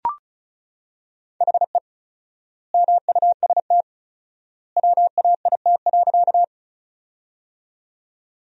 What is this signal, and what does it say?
Morse code, heard as a keyed tone, about 34 wpm, 700 Hz. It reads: HE MUST WAIT.
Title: 4x Word-Spacing - Sentences from Top 500 Words - 35wpm